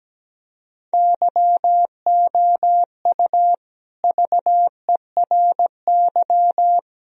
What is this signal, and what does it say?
Morse code, heard as a keyed tone, about 17 wpm, 700 Hz. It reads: YOU VERY
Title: Speed-Racing - Sets of 4 Words 17wpm